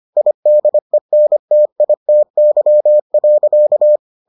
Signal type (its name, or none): Morse code